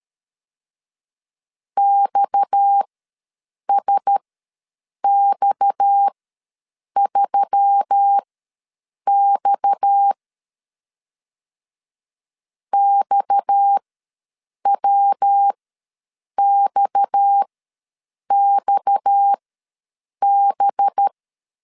Morse code, heard as a keyed tone, about 13 wpm, spaced as for 7 wpm, 800 Hz. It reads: XSX3X XWXXB
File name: LEZIONI / 6 PKX0 / 6 PKX ws 12wpm 3106